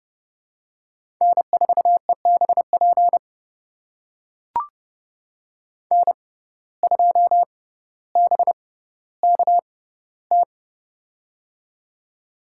Morse code, heard as a keyed tone, about 30 wpm, spaced as for 15 wpm, 700 Hz. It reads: N4EBP N2BKT
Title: USA Call Signs - 5 Characters - 6x Character Spacing - 30wpm